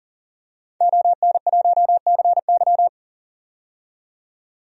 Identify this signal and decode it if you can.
Morse code, heard as a keyed tone, about 40 wpm, 700 Hz. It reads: ON1CY
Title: DXCC-100 Call Signs - 5 Characters - 40wpm